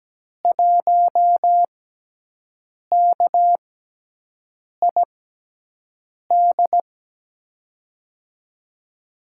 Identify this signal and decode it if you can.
Morse code, heard as a keyed tone, about 17 wpm, spaced as for 6 wpm, 700 Hz. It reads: 1KID
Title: USA Call Signs - 6x Character Spacing - 17wpm